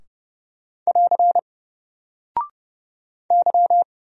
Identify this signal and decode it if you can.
Morse code, heard as a keyed tone, about 30 wpm, 700 Hz. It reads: <AR> Y
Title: Single Character (Letter:Number:Punctuation:Prosign) Alt 30wpm